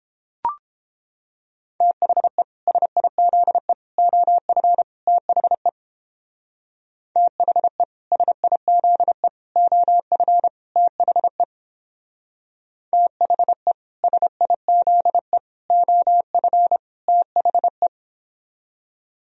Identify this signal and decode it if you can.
Morse code, heard as a keyed tone, about 29 wpm, 700 Hz. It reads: THE SIZE OF THE THE SIZE OF THE THE SIZE OF THE